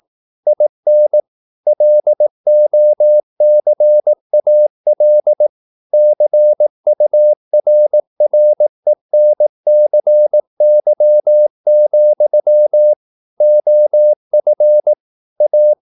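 Morse code, heard as a keyed tone, about 18 wpm, 600 Hz. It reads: IN LOCAL CURRENCY, OF A
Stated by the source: Morse practice at 18 wpm, English text, from CW med Gnister.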